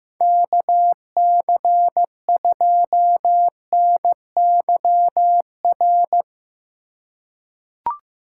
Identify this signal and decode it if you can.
Morse code, heard as a keyed tone, about 15 wpm, 700 Hz. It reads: KC2NYR
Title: USA Call Signs - 6 Characters - 2x Character Spacing - 15wpm